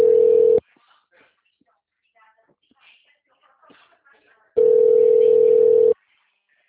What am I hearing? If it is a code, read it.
Ringback tone (North America), 440 + 480 Hz, with speech.
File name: out- -2011- - - .99496